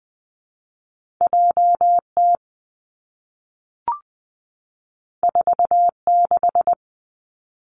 Morse code, heard as a keyed tone, about 20 wpm, 700 Hz. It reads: J T 4 6
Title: Two Challenging Letter-Number Combinations 20wpm